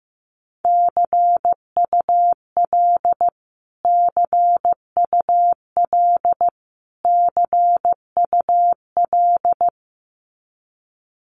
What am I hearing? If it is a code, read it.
Morse code, heard as a keyed tone, about 15 wpm, 700 Hz. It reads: CUL CUL CUL
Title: Abbreviations Repeated 3x - 15wpm